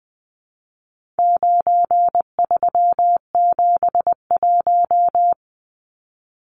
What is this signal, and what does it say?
Morse code, heard as a keyed tone, about 20 wpm, 700 Hz. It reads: 9371
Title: Numbers - 4-Digit 20wpm